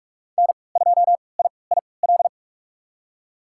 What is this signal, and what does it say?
Morse code, heard as a keyed tone, about 45 wpm, spaced as for 25 wpm, 700 Hz. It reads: N2IIL
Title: USA Call Signs - 5 Characters - 3x Character Spacing - 45wpm